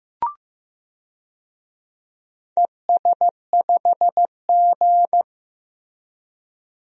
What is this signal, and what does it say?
Morse code, heard as a keyed tone, about 15 wpm, 700 Hz. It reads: ES5G
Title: DXCC-100 Call Signs - 4 Characters - Repeated 3x - 15wpm